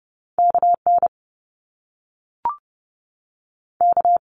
Morse code, heard as a keyed tone, about 30 wpm, 700 Hz. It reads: K N K
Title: Top Two Letter Combinations in Words 30wpm